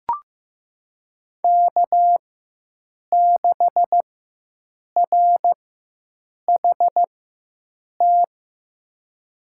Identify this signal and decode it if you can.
Morse code, heard as a keyed tone, about 15 wpm, spaced as for 7 wpm, 700 Hz. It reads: K6RHT
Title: USA Call Signs - 4x Character Spacing - 15wpm